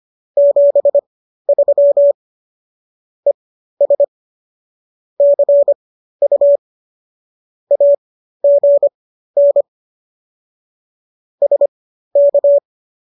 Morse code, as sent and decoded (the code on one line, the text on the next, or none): --... ...-- / . ... / -.-. ..- / .- --. -. / ... -.-
73 ES CU AGN SK